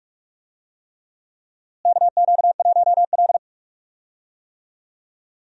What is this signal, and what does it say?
Morse code, heard as a keyed tone, about 46 wpm, 700 Hz. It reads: KQ1L
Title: USA Call Signs - 4 Characters - Repeated 3x - 45wpm